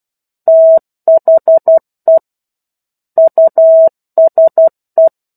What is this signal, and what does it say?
Morse code, heard as a keyed tone, about 12 wpm, 650 Hz. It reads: THE USE
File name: 01-Chapter-01 12 WPM